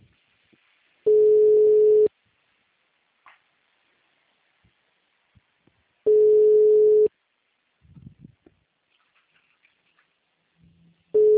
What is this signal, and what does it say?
Ringback tone (Europe, 425 Hz).